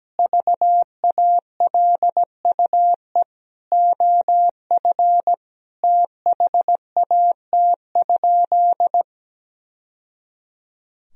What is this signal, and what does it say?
Morse code, heard as a keyed tone, about 17 wpm, 700 Hz. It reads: VALUE OF THAT?
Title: Sentences from Top 1000 Words - Review Part 3 - 17wpm